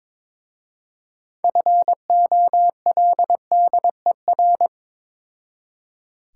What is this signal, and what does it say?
Morse code, heard as a keyed tone, about 23 wpm, 700 Hz. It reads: FOLDER
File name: Difficult Words with Hints - 22wpm